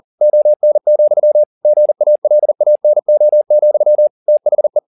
Morse code, heard as a keyed tone, about 40 wpm, 600 Hz. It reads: ON, GALANO, THE
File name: the old man and the sea 2 40wpm